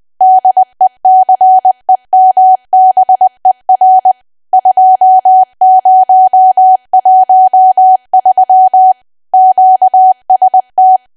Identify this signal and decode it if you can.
Morse code, heard as a keyed tone, about 20 wpm, 750 Hz. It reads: DECEMBER 2013 QST
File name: cwa / W1AW-Level3 / 20WPM